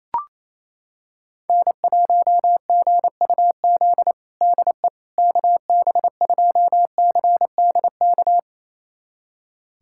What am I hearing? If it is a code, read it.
Morse code, heard as a keyed tone, about 28 wpm, 700 Hz. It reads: N1GUZ DE KB2CDK
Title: USA Callsign DE Callsign 28wpm